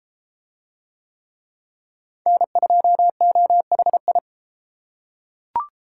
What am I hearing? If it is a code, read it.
Morse code, heard as a keyed tone, about 31 wpm, 700 Hz. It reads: N2OHI